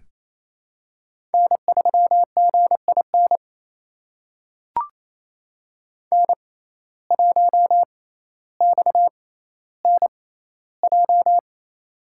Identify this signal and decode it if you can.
Morse code, heard as a keyed tone, about 28 wpm, spaced as for 14 wpm, 700 Hz. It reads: N3GIN N1XNJ
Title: USA Call Signs - 5 Characters - 6x Character Spacing - 28wpm